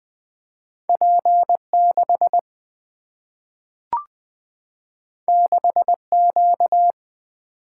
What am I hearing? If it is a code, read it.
Morse code, heard as a keyed tone, about 20 wpm, 700 Hz. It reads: P 6 6 Q
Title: Single-Letter Single-Number 20wpm